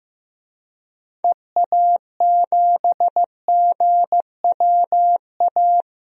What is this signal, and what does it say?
Morse code, heard as a keyed tone, about 15 wpm, 700 Hz. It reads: EA7GWA
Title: DXCC-100 Call Signs - 6 Characters - Repeated 3x - 15wpm